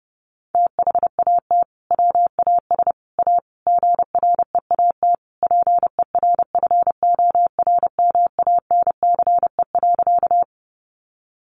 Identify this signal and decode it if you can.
Morse code, heard as a keyed tone, about 30 wpm, 700 Hz. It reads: THAT WAS A GREAT PERFORMANCE.
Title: Sentences from Top 1000 Words 30wpm